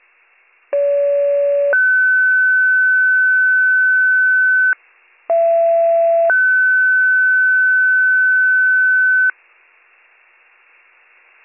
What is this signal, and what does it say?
Two-tone page, heard as a one-second tone, then a three-second tone, about 570 then 1500 Hz, about 670 then 1500 Hz. It is heard from 0.7 to 9.3 s.